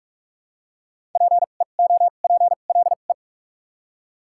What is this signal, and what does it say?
Morse code, heard as a keyed tone, about 46 wpm, spaced as for 33 wpm, 700 Hz. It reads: PEOPLE
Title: 2x Character Spacing - Top 100 Words - 45wpm